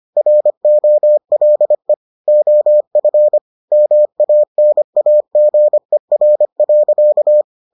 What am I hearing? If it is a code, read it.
Morse code, heard as a keyed tone, about 25 wpm, 600 Hz. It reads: ROLE OF MANAGER.